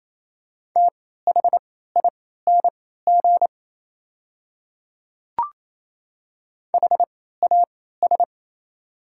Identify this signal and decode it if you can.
Morse code, heard as a keyed tone, about 27 wpm, spaced as for 16 wpm, 690 Hz. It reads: THING HAS
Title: 3x Character Spacing - Top 100 Words - 28wpm